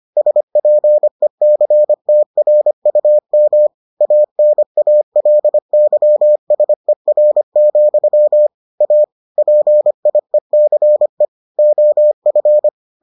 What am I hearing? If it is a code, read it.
Morse code, heard as a keyed tone, about 25 wpm, 600 Hz. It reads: SPECTRUM ANALYSER, A PIECE OF